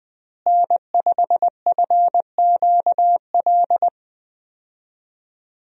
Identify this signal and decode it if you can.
Morse code, heard as a keyed tone, about 20 wpm, 700 Hz. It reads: N5FQL
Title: USA Call Signs - 5 Characters - Repeated 2x - 20wpm